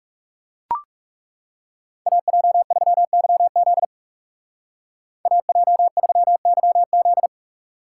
Morse code, heard as a keyed tone, about 39 wpm, 700 Hz. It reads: AJ3YZ AJ3YZ